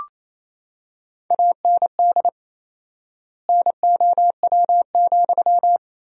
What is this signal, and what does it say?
Morse code, heard as a keyed tone, about 27 wpm, 700 Hz. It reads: AND NOW,
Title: 4x Word-Spacing - Sentences from Top 300 Words - 28wpm